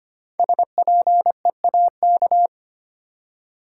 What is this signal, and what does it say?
Morse code, heard as a keyed tone, about 25 wpm, 700 Hz. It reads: SPEAK
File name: 5 Letter Words - First 3 Letters Given - 25wpm